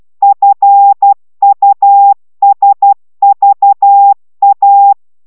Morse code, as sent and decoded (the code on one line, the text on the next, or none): ..-. ..- ... ...- .-
FUSVA